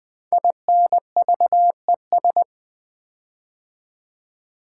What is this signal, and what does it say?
Morse code, heard as a keyed tone, about 20 wpm, 700 Hz. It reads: INVES
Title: Build A Word - 5 Letters At A Time - 10-Letter Words - 20wpm